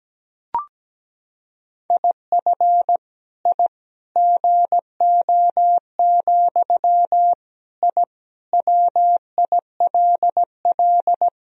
Morse code, heard as a keyed tone, about 17 wpm, 700 Hz. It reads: IF I GO, I WILL
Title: Sentences from Top 1000 Words - Review Part 3 - 17wpm